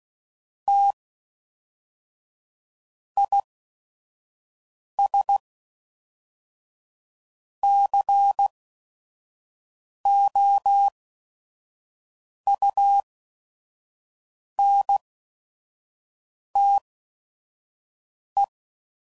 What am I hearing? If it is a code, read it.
Morse code, heard as a keyed tone, about 16 wpm, 800 Hz. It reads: T I S C O U N T E